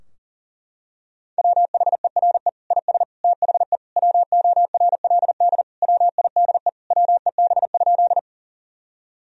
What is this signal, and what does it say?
Morse code, heard as a keyed tone, about 40 wpm, 700 Hz. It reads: WHERE IS THE WORLD WIDE WEB?
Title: Sentences from Top 1000 Words - Review Part 4 - 40wpm